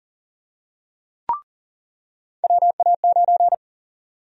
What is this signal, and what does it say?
Morse code, heard as a keyed tone, about 40 wpm, 700 Hz. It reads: WA9
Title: USA Call Sign Prefixes - 40wpm